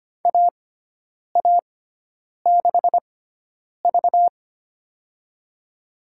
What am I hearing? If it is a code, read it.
Morse code, heard as a keyed tone, about 25 wpm, spaced as for 9 wpm, 700 Hz. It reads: AA6V
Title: USA Call Signs - 4 Characters - 6x Character Spacing - 25wpm